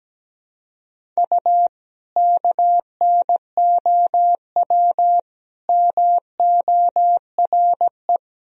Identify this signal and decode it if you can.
Morse code, heard as a keyed tone, about 17 wpm, 700 Hz. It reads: U KNOW MORE
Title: Sentences from Top 100 Words and Abbreviations 17wpm